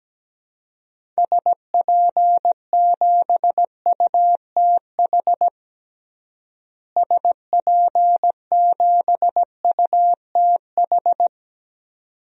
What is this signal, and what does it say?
Morse code, heard as a keyed tone, about 17 wpm, 700 Hz. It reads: SP7UTH SP7UTH
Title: DXCC-100 Call Signs - 6 Characters - Repeated 2x - 17wpm